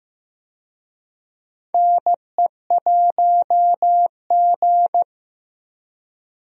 Morse code, heard as a keyed tone, about 15 wpm, 700 Hz. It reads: NE1G
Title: USA Call Signs - 4 Characters - Repeated 2x - 15wpm